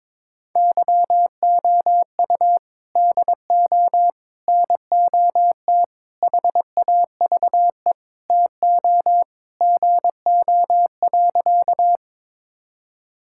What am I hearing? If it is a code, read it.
Morse code, heard as a keyed tone, about 22 wpm, 700 Hz. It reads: YOU DO NOT HAVE TO GO.
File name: practice / 22wpm / Sentences with 4-Word Phrases - 22wpm